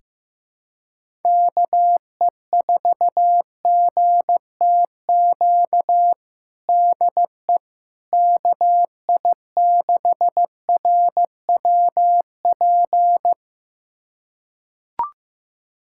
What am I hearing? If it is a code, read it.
Morse code, heard as a keyed tone, about 15 wpm, 700 Hz. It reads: KE4GTQ DE KI6RWP
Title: USA Callsign DE Callsign - 6 Characters - 15wpm